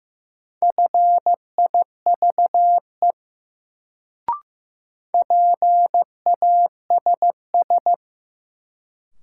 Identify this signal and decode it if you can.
Morse code, heard as a keyed tone, about 15 wpm, 700 Hz. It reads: FIVE PASS
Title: Sentences from Top 400 Words - New Word Drill - 15wpm